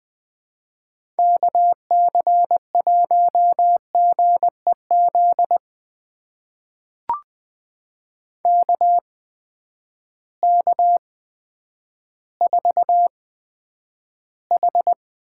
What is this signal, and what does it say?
Morse code, heard as a keyed tone, about 20 wpm, spaced as for 10 wpm, 700 Hz. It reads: KC1GEZ KK4H
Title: USA Call Signs - 6 Characters - 8x Character Spacing - 20wpm